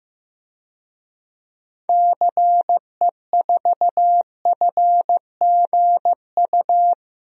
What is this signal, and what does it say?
Morse code, heard as a keyed tone, about 15 wpm, 700 Hz. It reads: CE4FGU
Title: DXCC-100 Call Signs - 6 Characters - 15wpm